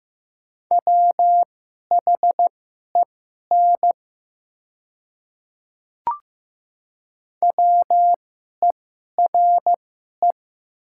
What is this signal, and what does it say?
Morse code, heard as a keyed tone, about 15 wpm, spaced as for 11 wpm, 700 Hz. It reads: WHEN WERE
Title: 2x Character Spacing - Top 200 Words - 15wpm